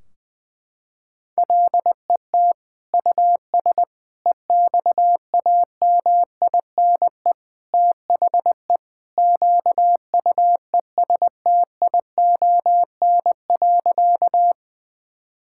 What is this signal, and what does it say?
Morse code, heard as a keyed tone, about 21 wpm, 700 Hz. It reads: LET US EXAMINE THE QUESTION.
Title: Sentences from Top 1600 Words 20wpm